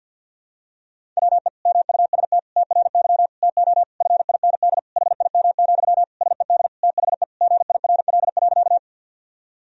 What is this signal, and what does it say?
Morse code, heard as a keyed tone, about 51 wpm, 700 Hz. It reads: WE MUST TRY TO FIND HIM, SED THE GIRL.
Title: Sentences from Top 400 Words and Abbreviations 50wpm